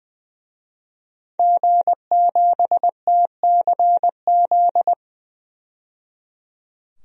Morse code, heard as a keyed tone, about 21 wpm, 700 Hz. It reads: G7TCZ
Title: DXCC-100 Call Signs - 5 Characters - 20wpm